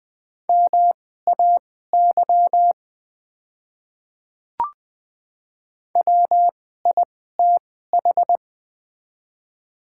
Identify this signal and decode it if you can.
Morse code, heard as a keyed tone, about 20 wpm, spaced as for 14 wpm, 690 Hz. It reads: MAY WITH
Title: 2x Character Spacing - Top 500 Words - 20wpm